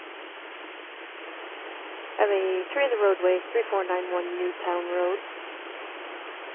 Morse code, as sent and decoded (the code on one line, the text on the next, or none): none